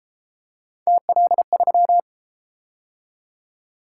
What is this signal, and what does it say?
Morse code, heard as a keyed tone, about 31 wpm, 700 Hz. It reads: TL3